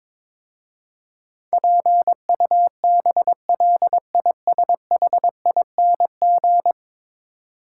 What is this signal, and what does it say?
Morse code, heard as a keyed tone, about 22 wpm, 700 Hz. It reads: PUBLISHING
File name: practice / 22wpm / 10 Letter Words - First 3 Letters Given - 22wpm